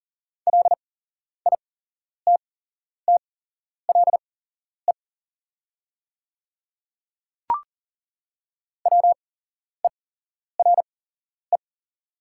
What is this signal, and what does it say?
Morse code, heard as a keyed tone, about 39 wpm, spaced as for 11 wpm, 700 Hz. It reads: LITTLE WERE
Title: 8x Character Spacing - Top 300 Words - 40wpm